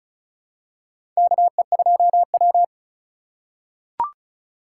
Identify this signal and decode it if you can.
Morse code, heard as a keyed tone, about 36 wpm, 700 Hz. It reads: KE2W
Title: USA Call Signs - 4 Characters - 3x Character Spacing - 35wpm